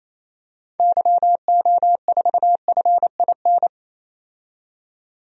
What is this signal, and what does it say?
Morse code, heard as a keyed tone, about 28 wpm, 700 Hz. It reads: YO4FIN
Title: DXCC-100 Call Signs - 6 Characters - 28wpm